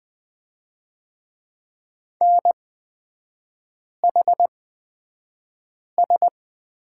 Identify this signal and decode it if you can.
Morse code, heard as a keyed tone, about 20 wpm, spaced as for 5 wpm, 700 Hz. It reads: NHS
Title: 243 Lesson 39e - Letter z - Three Characters 20wpm